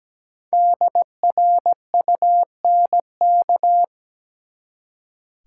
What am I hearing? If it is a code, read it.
Morse code, heard as a keyed tone, about 17 wpm, 700 Hz. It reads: DRUNK
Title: Five Letter Words - 17wpm